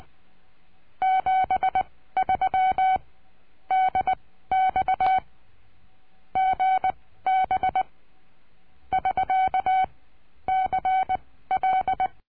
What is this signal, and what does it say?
Morse code, heard as a keyed tone, about 20 wpm, spaced as for 14 wpm, 760 Hz. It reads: 73 DX GB <SK> CL
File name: cursos / CURSO DE TELEGRAFIA / Telegrafia - Trilha 22